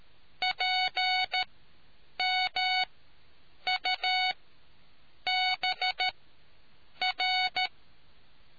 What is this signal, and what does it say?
Morse code, heard as a keyed tone, about 13 wpm, spaced as for 7 wpm, 740 Hz. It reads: PMUBR